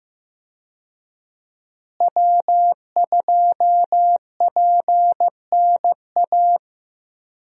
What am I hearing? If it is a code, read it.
Morse code, heard as a keyed tone, about 15 wpm, 700 Hz. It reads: W2PNA
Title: USA Call Signs - 5 Characters - Repeated 2x - 15wpm